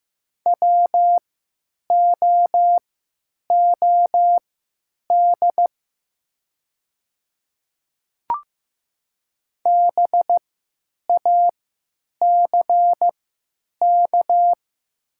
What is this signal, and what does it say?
Morse code, heard as a keyed tone, about 15 wpm, spaced as for 9 wpm, 700 Hz. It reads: WOOD BACK
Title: 3x Character Spacing - Top 300 Words - 15wpm